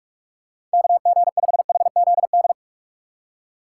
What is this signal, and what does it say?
Morse code, heard as a keyed tone, about 46 wpm, 700 Hz. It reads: KG5HZD
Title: USA Call Signs - 6 Characters - 4x Character Spacing - 45wpm